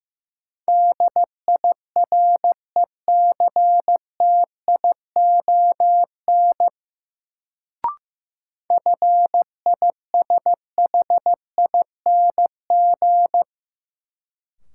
Morse code, heard as a keyed tone, about 15 wpm, 700 Hz. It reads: DIRECTION FISHING
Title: Sentences from Top 1000 Words - Review Part 5 - New Word Drill - 15wpm